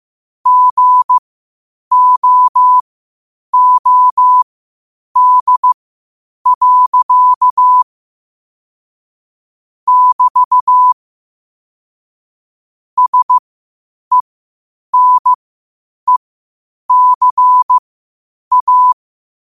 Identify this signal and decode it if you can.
Morse code, heard as a keyed tone, about 15 wpm, spaced as for 8 wpm, 1000 Hz. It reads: GOOD. <BT> SENECA
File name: qotdfiles / 8wpm